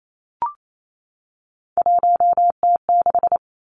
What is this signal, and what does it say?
Morse code, heard as a keyed tone, about 28 wpm, 700 Hz. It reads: 1T6
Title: Three Challenging Letter-Number Combinations 28wpm